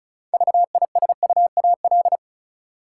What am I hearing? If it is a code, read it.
Morse code, heard as a keyed tone, about 34 wpm, 700 Hz. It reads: VISUAL
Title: Six Letter Words - 35wpm